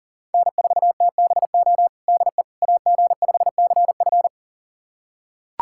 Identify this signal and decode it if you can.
Morse code, heard as a keyed tone, about 40 wpm, 700 Hz. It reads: N4TBO DE AG5CF
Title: USA Callsign DE Callsign - 5 Characters - 40wpm